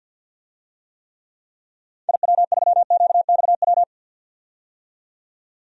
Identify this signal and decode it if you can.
Morse code, heard as a keyed tone, about 49 wpm, 700 Hz. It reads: IW3QXW
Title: DXCC-100 Call Signs - 6 Characters - 50wpm